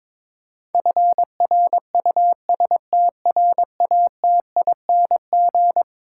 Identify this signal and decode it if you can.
Morse code, heard as a keyed tone, about 22 wpm, 700 Hz. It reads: FRUSTRATING
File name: practice / 22wpm / Eleven Letter Words - 22wpm